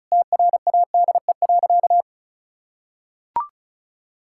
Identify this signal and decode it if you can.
Morse code, heard as a keyed tone, about 36 wpm, 700 Hz. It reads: TRADE.
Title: Sentences from Top 1000 Words - Review Part 5 - 35wpm